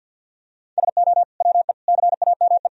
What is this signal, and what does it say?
Morse code, heard as a keyed tone, about 51 wpm, 700 Hz. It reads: SO WE CAME
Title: Sentences from Top 400 Words and Abbreviations 50wpm